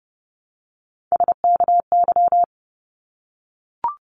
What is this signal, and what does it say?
Morse code, heard as a keyed tone, about 30 wpm, 700 Hz. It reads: SKY
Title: Multi-letter Phonemes 30wpm